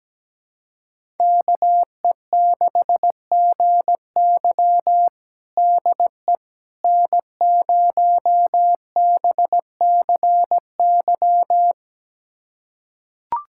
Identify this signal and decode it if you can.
Morse code, heard as a keyed tone, about 17 wpm, 700 Hz. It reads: KE6GY DE N0BCY
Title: USA Callsign DE Callsign 17wpm